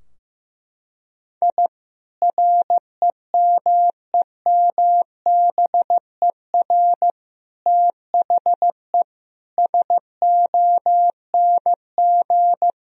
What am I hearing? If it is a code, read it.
Morse code, heard as a keyed tone, about 16 wpm, 700 Hz. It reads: I REMEMBER THE SONG
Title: Sentences from Top 1000 Words - Review Part 2 - 15wpm